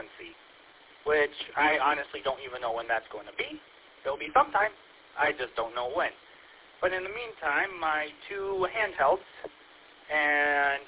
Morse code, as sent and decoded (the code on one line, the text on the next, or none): none